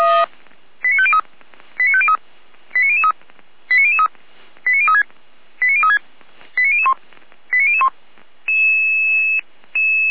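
Five-tone selective call (ZVEI), heard as quick runs of five tones, tones of about 70 ms. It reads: 89503 89503 89003 89003 89036 89036 89001 89001
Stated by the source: ZVEI five-tone selective call received by SDR, from fire brigade Feuerwehr Orlamünde.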